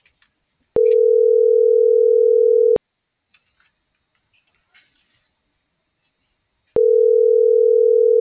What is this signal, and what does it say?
Ringback tone (North America), 440 + 480 Hz.